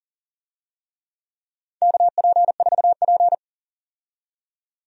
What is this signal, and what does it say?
Morse code, heard as a keyed tone, about 37 wpm, 700 Hz. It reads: KP4P
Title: USA Call Signs - 4 Characters - 40wpm